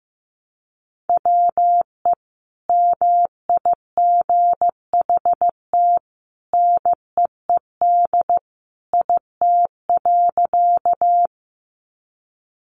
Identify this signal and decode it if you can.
Morse code, heard as a keyed tone, about 15 wpm, 700 Hz. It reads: WE MIGHT NEED IT.
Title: A Sound Of Thunder 15wpm